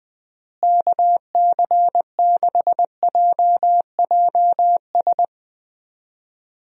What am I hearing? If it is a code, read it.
Morse code, heard as a keyed tone, about 19 wpm, 700 Hz. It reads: KC6JJS
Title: USA Call Signs - 6 Characters - Repeated 2x - 20wpm